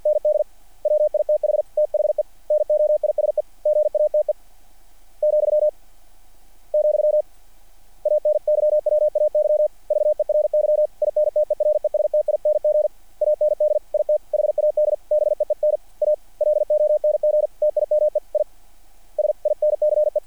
Morse code, heard as a keyed tone, about 50 wpm, 600 Hz. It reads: ND WITH THE NOISE GATE , , ANYWAY VERY INTERESTING AND IT HAD BEEN A LONG TIME ISINCE
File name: archived / CRSnet